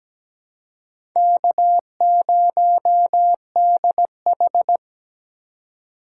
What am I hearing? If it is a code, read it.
Morse code, heard as a keyed tone, about 17 wpm, 700 Hz. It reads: K0DH
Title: USA Call Signs - 4 Characters - 3x Character Spacing - 17wpm